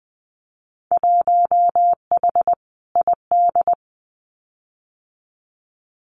Morse code, heard as a keyed tone, about 20 wpm, 700 Hz. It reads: 1H ID
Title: Winter Field Day Exchanges 20wpm